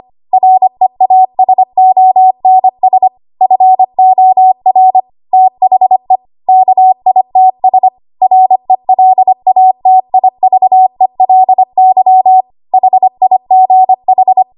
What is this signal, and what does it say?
Morse code, heard as a keyed tone, about 25 wpm, 750 Hz. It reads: REASONS FOR THE KITS RELATIVELY HIGH
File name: Archive / 25 WPM / 25WPM